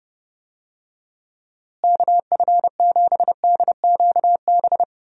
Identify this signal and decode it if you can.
Morse code, heard as a keyed tone, about 30 wpm, 700 Hz. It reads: KF7DQB
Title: USA Call Signs - 6 Characters - Repeated 2x - 30wpm